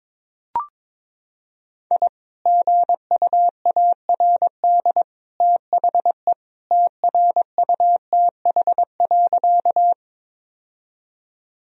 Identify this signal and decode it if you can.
Morse code, heard as a keyed tone, about 21 wpm, 700 Hz. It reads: I GUARD THE TRUTH.
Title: Sentences from Top 1000 Words - Review Part 4 - 22wpm